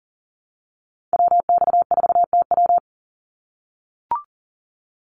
Morse code, heard as a keyed tone, about 42 wpm, 700 Hz. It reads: WX4TW
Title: NA SOTA Activator Callsigns 40wpm